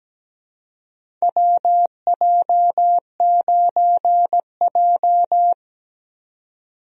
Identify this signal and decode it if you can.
Morse code, heard as a keyed tone, about 17 wpm, 700 Hz. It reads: WJ9J
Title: USA Call Signs - 4 Characters - 17wpm